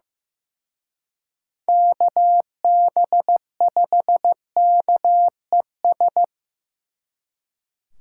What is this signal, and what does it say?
Morse code, heard as a keyed tone, about 18 wpm, 700 Hz. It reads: KB5KES
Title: Speed Racing - USA 2x3 Call Signs 15wpm